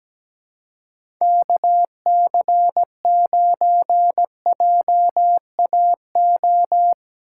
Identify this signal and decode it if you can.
Morse code, heard as a keyed tone, about 17 wpm, 700 Hz. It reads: KC9JAO
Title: USA Call Signs - 6 Characters - 6x Character Spacing - 17wpm